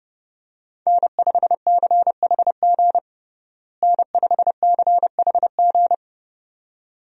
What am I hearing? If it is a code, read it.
Morse code, heard as a keyed tone, about 31 wpm, 700 Hz. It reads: N5CHG N5CHG
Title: USA Call Signs - 5 Characters - Repeated 2x - 30wpm